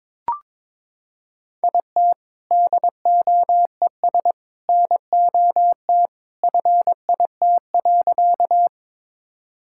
Morse code, heard as a keyed tone, about 22 wpm, 700 Hz. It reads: IT DOES NOT FIT.
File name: practice / 22wpm / Sentences with 2-Word Phrases - 22wpm